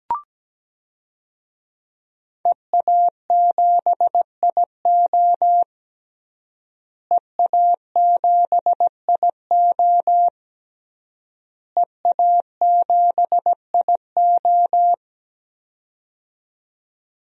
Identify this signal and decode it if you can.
Morse code, heard as a keyed tone, about 17 wpm, 700 Hz. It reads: EA7IO EA7IO EA7IO
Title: DXCC-100 Call Signs - 5 Characters - Repeated 3x - 17wpm